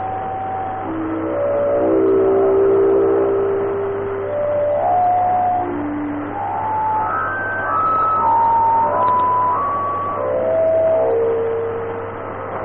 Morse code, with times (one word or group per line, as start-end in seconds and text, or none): none